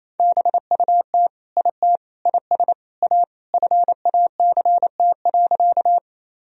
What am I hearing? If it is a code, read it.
Morse code, heard as a keyed tone, about 28 wpm, 700 Hz. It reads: BUT IT IS A FACT.
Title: Sentences from Top 1000 Words - Review Part 3 - 28wpm